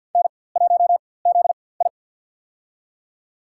Morse code, heard as a keyed tone, about 47 wpm, spaced as for 23 wpm, 700 Hz. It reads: N1ZI
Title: USA Call Signs - 4x Character Spacing - 50wpm